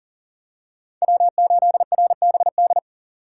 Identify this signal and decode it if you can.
Morse code, heard as a keyed tone, about 40 wpm, 700 Hz. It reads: W8RBD
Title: USA Call Signs - 5 Characters - 3x Character Spacing - 40wpm